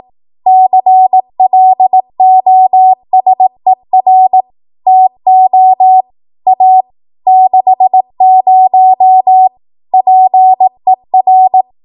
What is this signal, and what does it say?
Morse code, heard as a keyed tone, about 18 wpm, 750 Hz. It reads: CLOSER TO A 60 PER
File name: Archive / 18 WPM / 18WPM